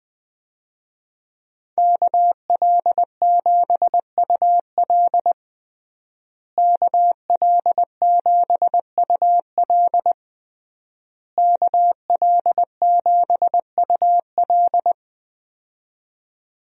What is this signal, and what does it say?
Morse code, heard as a keyed tone, about 20 wpm, 700 Hz. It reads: KL7UL KL7UL KL7UL
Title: USA Call Signs - 5 Characters - Repeated 3x - 20wpm